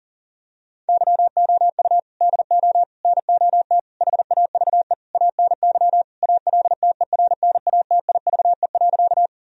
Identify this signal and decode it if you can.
Morse code, heard as a keyed tone, about 37 wpm, 700 Hz. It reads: YOU DO NOT HAVE ANY ALTERNATIVE.
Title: Sentences from Top 1000 Words - Review Part 4 - 40wpm